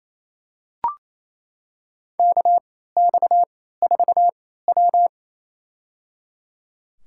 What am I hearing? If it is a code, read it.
Morse code, heard as a keyed tone, about 28 wpm, spaced as for 16 wpm, 700 Hz. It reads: KX4W
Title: USA Call Signs - 3x Character Spacing - 28wpm